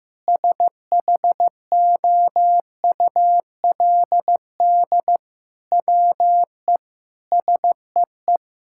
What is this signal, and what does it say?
Morse code, heard as a keyed tone, about 15 wpm, 700 Hz. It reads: SHOULD WE SEE
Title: Sentences from Top 1000 Words - Review Part 5 - 15wpm